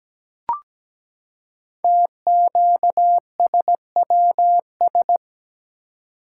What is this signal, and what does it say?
Morse code, heard as a keyed tone, about 17 wpm, 700 Hz. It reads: TQSWS
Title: Code Groups - 5 Letters - 17wpm